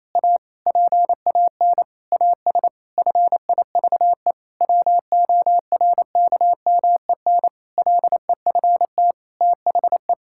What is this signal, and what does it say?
Morse code, heard as a keyed tone, about 28 wpm, 700 Hz. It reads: A PAN AS FIVE WORKMEN LEFT THE
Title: Pangram Challenge v2 - 28wpm